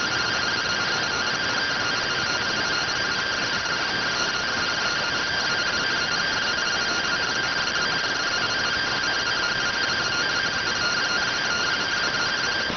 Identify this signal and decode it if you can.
Two-tone teleprinter signal, about 170 Hz shift.